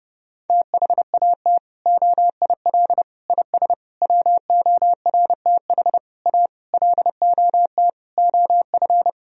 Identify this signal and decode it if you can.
Morse code, heard as a keyed tone, about 32 wpm, 700 Hz. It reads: THAT OIL IS WORTH A LOT OF
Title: Sentences from Top 1000 Words - Review Part 4 - 30wpm